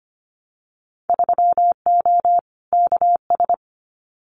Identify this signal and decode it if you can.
Morse code, heard as a keyed tone, about 25 wpm, 700 Hz. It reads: 3O KS
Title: Winter Field Day Exchanges 25wpm